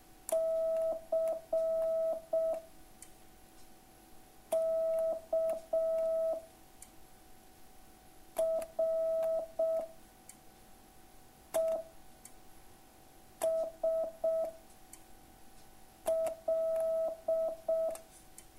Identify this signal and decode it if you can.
Morse code, heard as a keyed tone, about 6 wpm, 650 Hz. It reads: C K R E S L